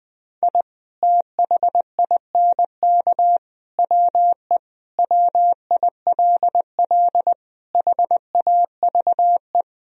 Morse code, heard as a keyed tone, about 20 wpm, 700 Hz. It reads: I THINK WE WILL HAVE
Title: Sentences from Top 1000 Words - Review Part 1 - 20wpm